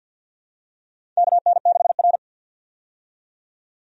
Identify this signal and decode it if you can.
Morse code, heard as a keyed tone, about 52 wpm, 700 Hz. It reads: KN6R
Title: USA Call Signs - 4 Characters - 50wpm